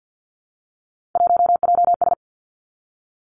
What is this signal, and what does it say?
Morse code, heard as a keyed tone, about 50 wpm, 700 Hz. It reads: 1JS
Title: Three Challenging Letter-Number Combinations 50wpm